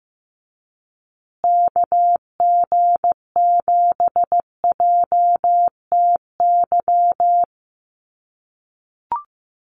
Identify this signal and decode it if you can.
Morse code, heard as a keyed tone, about 15 wpm, 700 Hz. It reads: KG7JTY
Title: USA Call Signs Repeated 2x 15wpm